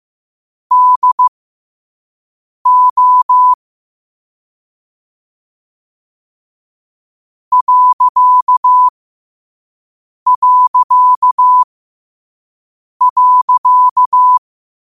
Morse code, heard as a keyed tone, about 15 wpm, spaced as for 5 wpm, 1000 Hz. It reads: DO ...